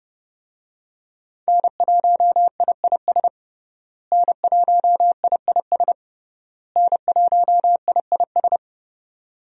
Morse code, heard as a keyed tone, about 30 wpm, 700 Hz. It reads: N1IIS N1IIS N1IIS
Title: USA Call Signs - 5 Characters - Repeated 3x - 30wpm